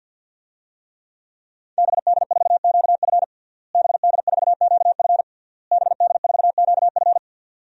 Morse code, heard as a keyed tone, about 52 wpm, 700 Hz. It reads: BD4QF BD4QF BD4QF
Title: DXCC-100 Call Signs - 5 Characters - Repeated 3x - 50wpm